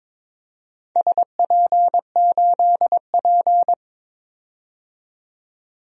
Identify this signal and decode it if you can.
Morse code, heard as a keyed tone, about 22 wpm, 700 Hz. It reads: SP8P
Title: DXCC-100 Call Signs - 4 Characters - 22wpm